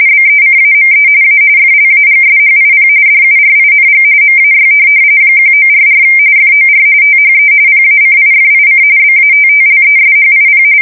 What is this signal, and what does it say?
RTTY, about 45.45 baud. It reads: BECAME A FLAGSHIP DURING MAY 1944 FOR THE PACIFIC BATTLESHIP
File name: fsk170 / USS-New-Jersey